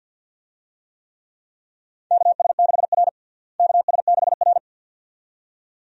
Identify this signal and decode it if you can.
Morse code, heard as a keyed tone, about 49 wpm, 700 Hz. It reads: KS6R KS6R
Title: USA Call Signs - 4 Characters - Repeated 2x - 50wpm